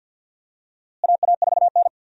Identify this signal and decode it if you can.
Morse code, heard as a keyed tone, about 53 wpm, 700 Hz. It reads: AA4N
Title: USA Call Signs - 4 Characters - Repeated 3x - 50wpm